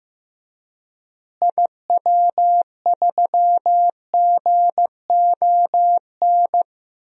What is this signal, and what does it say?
Morse code, heard as a keyed tone, about 15 wpm, 700 Hz. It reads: IW3GON
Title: DXCC-100 Call Signs - 6 Characters - Repeated 3x - 15wpm